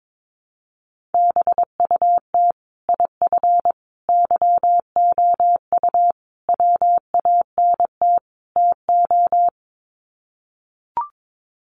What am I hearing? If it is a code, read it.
Morse code, heard as a keyed tone, about 22 wpm, 700 Hz. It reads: BUT IF YOU WANT TO
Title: Sets of 5 Words Encore2 22wpm